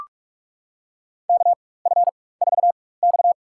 Morse code, heard as a keyed tone, about 41 wpm, spaced as for 20 wpm, 700 Hz. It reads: KF4X
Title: USA Call Signs - 4 Characters - 4x Character Spacing - 45wpm